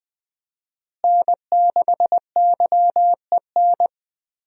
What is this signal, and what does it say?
Morse code, heard as a keyed tone, about 20 wpm, 700 Hz. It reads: N6YEN
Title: USA Call Signs - 5 Characters - Repeated 3x - 20wpm